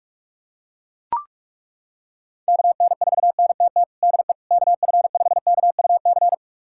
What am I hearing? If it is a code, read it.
Morse code, heard as a keyed tone, about 41 wpm, 700 Hz. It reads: KN4NTT DE KF5KUC
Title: USA Callsign DE Callsign 45wpm